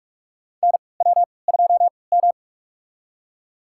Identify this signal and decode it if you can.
Morse code, heard as a keyed tone, about 41 wpm, spaced as for 25 wpm, 700 Hz. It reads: NW2M
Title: USA Call Signs - 4 Characters - 3x Character Spacing - 45wpm